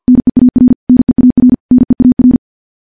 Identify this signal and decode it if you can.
Morse code, heard as a keyed tone, about 51 wpm, 250 Hz. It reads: NEAR NEAR NEAR
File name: morse / morse-training-50WPM